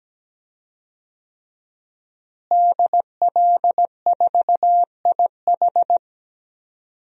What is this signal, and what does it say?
Morse code, heard as a keyed tone, about 17 wpm, 700 Hz. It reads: DL4IH
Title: DXCC-100 Call Signs - 5 Characters - Repeated 3x - 17wpm